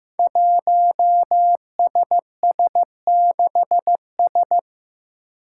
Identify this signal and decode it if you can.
Morse code, heard as a keyed tone, about 15 wpm, 700 Hz. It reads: 1SS6S
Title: Five Challenging Letter-Number Combinations 15wpm